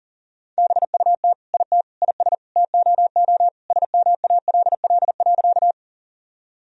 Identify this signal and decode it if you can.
Morse code, heard as a keyed tone, about 40 wpm, 700 Hz. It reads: BUT IT IS TOO SMALL.
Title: Sentences from Top 1000 Words - Review Part 1 - 40wpm